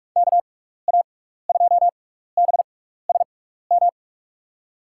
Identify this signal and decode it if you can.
Morse code, heard as a keyed tone, about 46 wpm, spaced as for 16 wpm, 700 Hz. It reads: KA2BSM
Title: USA Call Signs - 6x Character Spacing - 45wpm